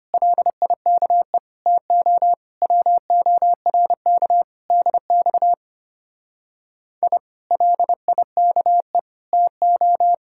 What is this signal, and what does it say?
Morse code, heard as a keyed tone, about 31 wpm, 700 Hz. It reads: LIKE TO WORK DX I LIKE TO
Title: Speed-Racing Common QSO Phrases - 25wpm